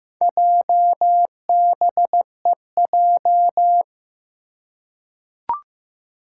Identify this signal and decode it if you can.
Morse code, heard as a keyed tone, about 15 wpm, 700 Hz. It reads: JBEJ
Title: Four Challenging Letter-Number Combinations 15wpm